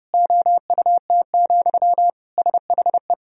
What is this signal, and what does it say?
Morse code, heard as a keyed tone, about 32 wpm, 700 Hz. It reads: OUT, SHE
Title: Sentences from Top 1000 Words - Review Part 3 - 30wpm